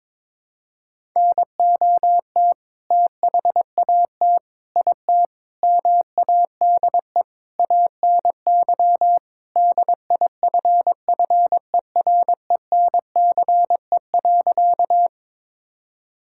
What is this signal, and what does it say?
Morse code, heard as a keyed tone, about 22 wpm, 700 Hz. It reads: NOT THAT IT MADE ANY DIFFERENCE.
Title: Sentences from Top 1000 Words - Review Part 5 - 22wpm